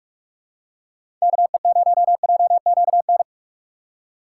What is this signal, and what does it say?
Morse code, heard as a keyed tone, about 43 wpm, 700 Hz. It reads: KE0JQN
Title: USA Call Signs - 6 Characters - 4x Character Spacing - 45wpm